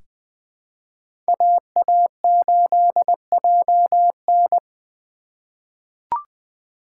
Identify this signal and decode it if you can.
Morse code, heard as a keyed tone, about 20 wpm, 700 Hz. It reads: AA8JN
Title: USA Call Signs - 2x Character Spacing - 20wpm